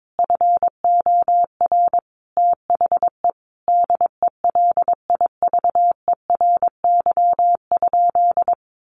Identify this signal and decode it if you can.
Morse code, heard as a keyed tone, about 22 wpm, 700 Hz. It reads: FOR THE DELIVERY?
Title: Sentences from Top 1500 Words 22wpm